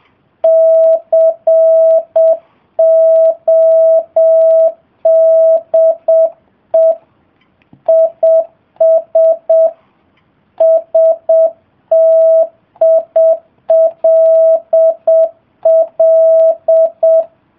Morse code, heard as a keyed tone, about 7 wpm, 650 Hz. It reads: CODE IS STILL